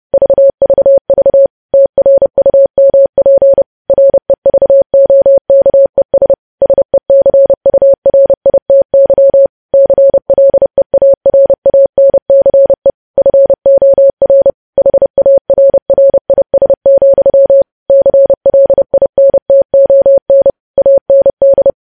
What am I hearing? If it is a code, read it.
Morse code, heard as a keyed tone, about 30 wpm, 550 Hz. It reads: VVV TRUMP REVOKES SECURITY CLEARANCE FOR HARRIS, CLINTON AND
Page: news headlines in morse code at 30 wpm